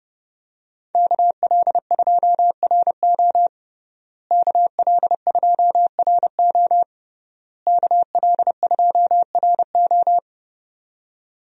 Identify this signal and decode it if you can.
Morse code, heard as a keyed tone, about 30 wpm, 700 Hz. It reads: KL2RO KL2RO KL2RO
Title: USA Call Signs - 5 Characters - Repeated 3x - 30wpm